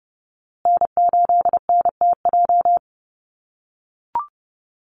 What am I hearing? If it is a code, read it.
Morse code, heard as a keyed tone, about 30 wpm, 700 Hz. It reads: N8NTJ
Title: USA Call Signs Repeated 3x 30wpm